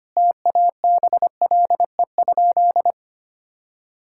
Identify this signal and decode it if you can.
Morse code, heard as a keyed tone, about 25 wpm, 700 Hz. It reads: TABLE?
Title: Sentences from Top 1000 Words - Review Part 6 - 25wpm